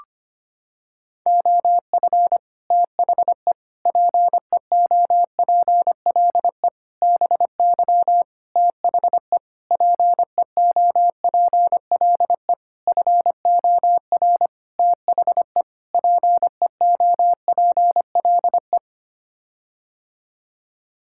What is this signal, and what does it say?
Morse code, heard as a keyed tone, about 25 wpm, 700 Hz. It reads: OF THE PEOPLE BY THE PEOPLE FOR THE PEOPLE
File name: Sets of 9 Words - 25wpm